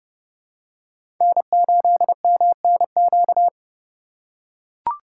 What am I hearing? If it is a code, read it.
Morse code, heard as a keyed tone, about 30 wpm, 700 Hz. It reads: N8MNQ
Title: USA Call Signs - 5 Characters - 3x Character Spacing - 30wpm